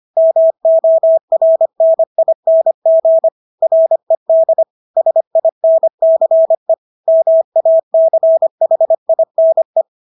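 Morse code, heard as a keyed tone, about 25 wpm, 650 Hz. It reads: MORNING RED SINCE MACHINE
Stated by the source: Morse practice of English words at 25 wpm from CW med Gnister.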